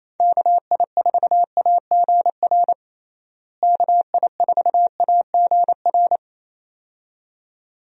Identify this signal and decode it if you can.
Morse code, heard as a keyed tone, about 28 wpm, 700 Hz. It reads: KI4AGR KI4AGR
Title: USA Call Signs - 6 Characters - Repeated 2x - 28wpm